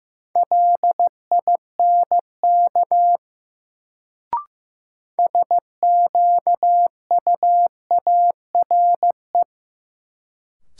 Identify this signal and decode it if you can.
Morse code, heard as a keyed tone, about 15 wpm, 700 Hz. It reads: LINK SQUARE
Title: Sentences from Top 1000 Words - Review Part 7 - New Word Drill - 15wpm